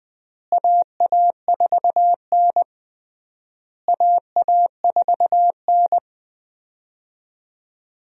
Morse code, heard as a keyed tone, about 21 wpm, 700 Hz. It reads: AA4N AA4N
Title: USA Call Signs - 4 Characters - Repeated 3x - 20wpm